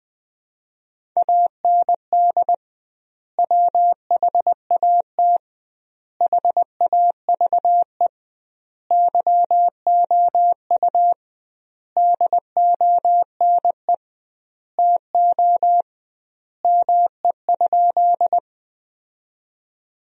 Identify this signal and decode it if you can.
Morse code, heard as a keyed tone, about 21 wpm, 700 Hz. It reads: AND WHAT HAVE YOU DONE TO ME?
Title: 2x Word-Spacing - Sentences from Top 500 Words - 20wpm